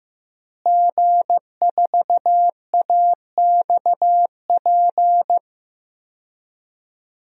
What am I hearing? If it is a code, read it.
Morse code, heard as a keyed tone, about 15 wpm, 700 Hz. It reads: G4AXP
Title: DXCC-100 Call Signs - 5 Characters - Repeated 2x - 15wpm